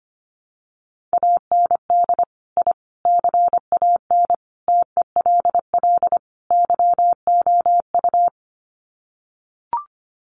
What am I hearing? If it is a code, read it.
Morse code, heard as a keyed tone, about 25 wpm, 700 Hz. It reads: AND I CAN TELL YOU
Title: Sets of 5 Words Encore2 25wpm